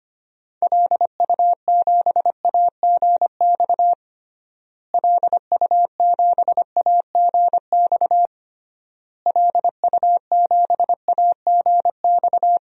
Morse code, heard as a keyed tone, about 25 wpm, 700 Hz. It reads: LU7AGX LU7AGX LU7AGX
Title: DXCC-100 Call Signs - 6 Characters - Repeated 3x - 25wpm